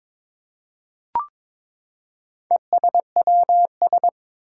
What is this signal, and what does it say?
Morse code, heard as a keyed tone, about 22 wpm, 700 Hz. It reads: ESWS
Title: Four Challenging Letter-Number Combinations 22wpm